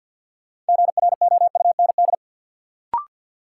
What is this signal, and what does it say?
Morse code, heard as a keyed tone, about 52 wpm, 700 Hz. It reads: GROUND
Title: Sets of 3 Words - New Word Drill - 50wpm